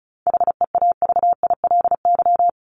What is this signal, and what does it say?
Morse code, heard as a keyed tone, about 34 wpm, 700 Hz. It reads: HEAVILY
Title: LY Suffix - Top 100 Words with Base Words 35wpm